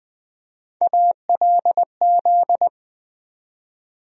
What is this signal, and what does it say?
Morse code, heard as a keyed tone, about 20 wpm, 700 Hz. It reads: ALZ
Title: USA Call Sign Postfixes - 3 Letters 20wpm